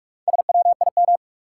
Morse code, heard as a keyed tone, about 46 wpm, 700 Hz. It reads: SWIM
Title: Four Letter Words - 45wpm